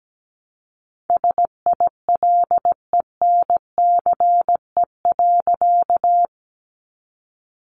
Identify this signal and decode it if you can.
Morse code, heard as a keyed tone, about 17 wpm, 700 Hz. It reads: SILENCE.
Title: A Sound Of Thunder 17wpm